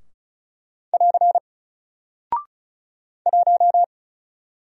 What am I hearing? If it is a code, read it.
Morse code, heard as a keyed tone, about 36 wpm, 700 Hz. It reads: <AR> 1
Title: Single Character (Letter:Number:Punctuation:Prosign) Alt 35wpm